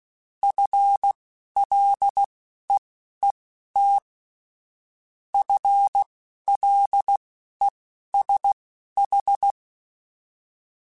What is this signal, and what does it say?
Morse code, heard as a keyed tone, about 16 wpm, spaced as for 11 wpm, 800 Hz. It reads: FLEET FLESH
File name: Spaced / FL Runs sp mp3s / FL Runs sp @15